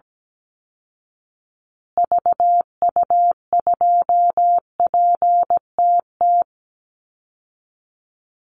Morse code, heard as a keyed tone, about 17 wpm, 700 Hz. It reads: VU2PTT
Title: Call Signs Repeated 3x - 17wpm